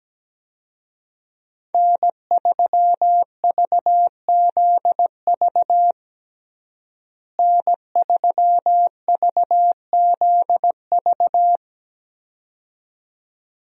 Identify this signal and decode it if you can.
Morse code, heard as a keyed tone, about 17 wpm, 700 Hz. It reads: N3VZV N3VZV
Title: USA Call Signs - 5 Characters - Repeated 2x - 17wpm